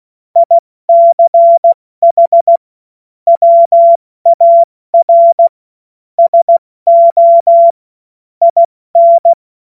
Morse code, heard as a keyed tone, about 16 wpm, spaced as for 14 wpm, 680 Hz. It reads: ICH WAR SO IN
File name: Witz-09-September-2025-70-F80